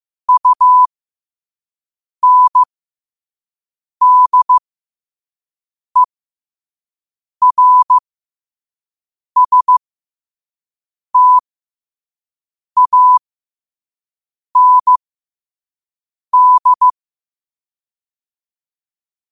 Morse code, heard as a keyed tone, about 15 wpm, spaced as for 5 wpm, 1000 Hz. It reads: UNDERSTAND
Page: 5 WPM morse code quotes